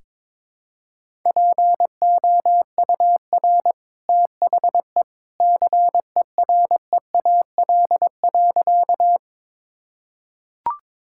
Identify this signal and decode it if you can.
Morse code, heard as a keyed tone, about 22 wpm, 700 Hz. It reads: POUR THE CEREAL.
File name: practice / 22wpm / Easy Sentences with Hints - 22wpm